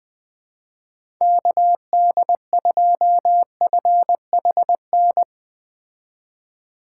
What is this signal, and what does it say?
Morse code, heard as a keyed tone, about 21 wpm, 700 Hz. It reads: KD2FHN